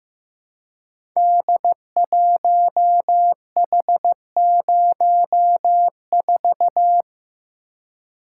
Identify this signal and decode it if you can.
Morse code, heard as a keyed tone, about 15 wpm, 700 Hz. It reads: D1H04
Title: Code Groups - 5 Letter:Numbers - 15wpm